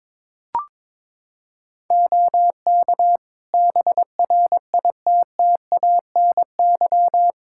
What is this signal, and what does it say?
Morse code, heard as a keyed tone, about 22 wpm, 700 Hz. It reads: OK BRITTANY
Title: K1USN SST QSO Practice 22wpm